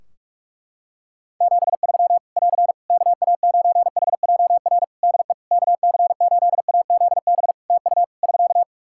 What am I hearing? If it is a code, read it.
Morse code, heard as a keyed tone, about 44 wpm, 700 Hz. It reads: 73 <AR> KA9HJR DE KC8AZB TU <SK>
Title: Practice QSO - Exchange 4 - Ending - 45wpm